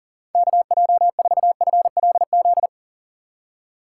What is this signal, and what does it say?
Morse code, heard as a keyed tone, about 40 wpm, 700 Hz. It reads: KJ4FLZ
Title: USA Call Signs - 6 Characters - 2x Character Spacing - 40wpm